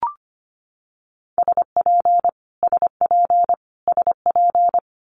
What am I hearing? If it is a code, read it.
Morse code, heard as a keyed tone, about 25 wpm, 700 Hz. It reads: SP SP SP
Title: Multi-letter Phonemes Repeated 3x 25wpm